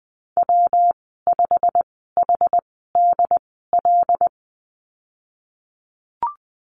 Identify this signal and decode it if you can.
Morse code, heard as a keyed tone, about 20 wpm, spaced as for 15 wpm, 700 Hz. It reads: W5HDL
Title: 154 Lesson 25g - Letter p - Call Signs 20wpm